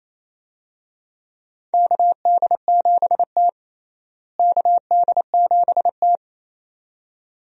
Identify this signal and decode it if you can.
Morse code, heard as a keyed tone, about 28 wpm, 700 Hz. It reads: KD7T KD7T
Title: USA Call Signs - 4 Characters - Repeated 2x - 28wpm